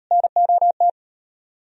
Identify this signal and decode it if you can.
Morse code, heard as a keyed tone, about 39 wpm, 700 Hz. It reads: NOT